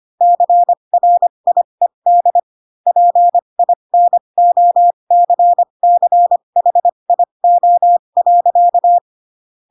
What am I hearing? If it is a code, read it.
Morse code, heard as a keyed tone, about 25 wpm, spaced as for 22 wpm, 700 Hz. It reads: CRIED PINOCCHIO.
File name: Pinocchio / Pinocchio ch4 18 25 ik1hge